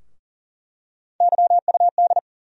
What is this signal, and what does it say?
Morse code, heard as a keyed tone, about 40 wpm, 700 Hz. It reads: YUD
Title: USA Call Sign Postfixes - 3 Letters 40wpm